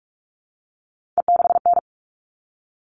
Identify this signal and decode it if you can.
Morse code, heard as a keyed tone, about 44 wpm, 700 Hz. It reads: E6N
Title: All Cut Numbers - 3-Digit 45wpm